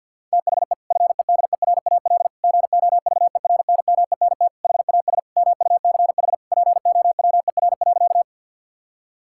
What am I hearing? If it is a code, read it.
Morse code, heard as a keyed tone, about 52 wpm, 700 Hz. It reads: THE FEDERAL GOVERNMENT HAS MUCH POWER.
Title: Sentences from Top 1000 Words - Review Part 5 - 50wpm